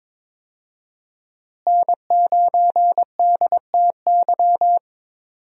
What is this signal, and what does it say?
Morse code, heard as a keyed tone, about 23 wpm, 700 Hz. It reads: N9DTY
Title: USA Call Signs - 5 Characters - Repeated 3x - 22wpm